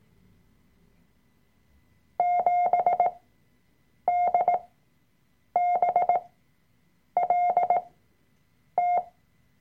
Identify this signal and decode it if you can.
Morse code, heard as a keyed tone, about 18 wpm, spaced as for 7 wpm, 710 Hz. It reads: 7DBLT
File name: Workshop Dateien / CW-Online-Kurs / Lektion 003